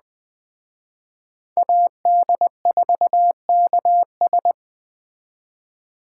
Morse code, heard as a keyed tone, about 20 wpm, 700 Hz. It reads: AD4KS
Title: Speed Racing - USA 2x2 Call Signs 17wpm